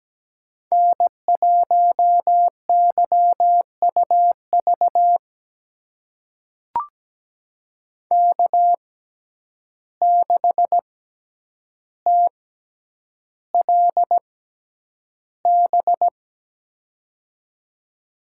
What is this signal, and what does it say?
Morse code, heard as a keyed tone, about 17 wpm, spaced as for 9 wpm, 700 Hz. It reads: N1YUV K6TLB
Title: USA Call Signs - 5 Characters - 6x Character Spacing - 17wpm